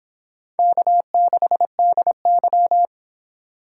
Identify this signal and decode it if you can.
Morse code, heard as a keyed tone, about 30 wpm, 700 Hz. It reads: K6DY